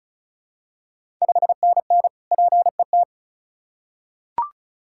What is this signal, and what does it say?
Morse code, heard as a keyed tone, about 36 wpm, 700 Hz. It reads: 5NN PET
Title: Ontario QSO Party Exchange 35wpm